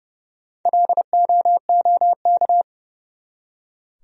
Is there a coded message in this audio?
Yes: Morse code from 0.7 to 2.6 s, heard as a keyed tone, about 40 wpm, 700 Hz. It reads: LOOK